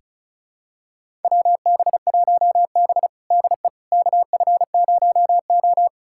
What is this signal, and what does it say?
Morse code, heard as a keyed tone, about 36 wpm, 700 Hz. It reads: WB1B DE KF0O
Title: USA Callsign DE Callsign - 4 Characters - 35wpm